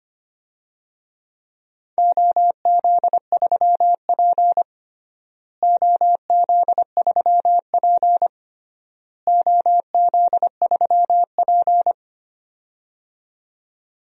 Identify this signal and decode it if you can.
Morse code, heard as a keyed tone, about 25 wpm, 700 Hz. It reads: OZ3P OZ3P OZ3P
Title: DXCC-100 Call Signs - 4 Characters - Repeated 3x - 25wpm